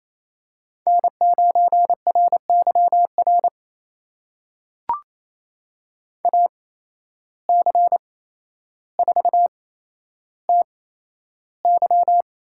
Morse code, heard as a keyed tone, about 28 wpm, spaced as for 12 wpm, 700 Hz. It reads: N9RYR AC4TY
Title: USA Call Signs - 5 Characters - 8x Character Spacing - 28wpm